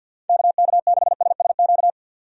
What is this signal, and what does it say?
Morse code, heard as a keyed tone, about 50 wpm, 700 Hz. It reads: KK6SSQ
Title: USA Call Signs - 6 Characters - 50wpm